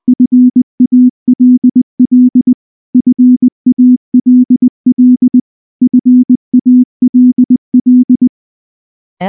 Morse code, heard as a keyed tone, about 19 wpm, 250 Hz. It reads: FALL FALL FALL
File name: morse / morse-training-20WPM-1